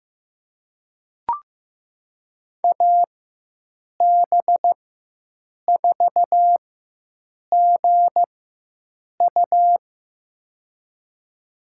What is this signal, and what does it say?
Morse code, heard as a keyed tone, about 15 wpm, spaced as for 7 wpm, 700 Hz. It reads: AB4GU
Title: USA Call Signs - 4x Character Spacing - 15wpm